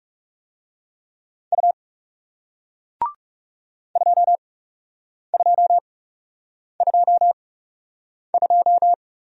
Morse code, heard as a keyed tone, about 40 wpm, spaced as for 8 wpm, 700 Hz. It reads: U 2222